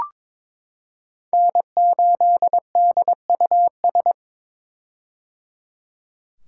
Morse code, heard as a keyed tone, about 22 wpm, 700 Hz. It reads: N8DUS
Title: K1USN SST QSO Practice - 22wpm